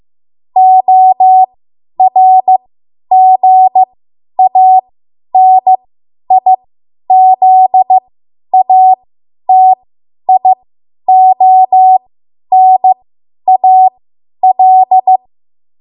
Morse code, heard as a keyed tone, about 15 wpm, spaced as for 10 wpm, 750 Hz. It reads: ORGANIZATIONAL
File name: Morse / 10WPM